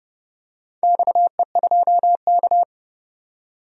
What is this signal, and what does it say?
Morse code, heard as a keyed tone, about 30 wpm, 700 Hz. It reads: XE2K
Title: DXCC-100 Call Signs - 4 Characters - 30wpm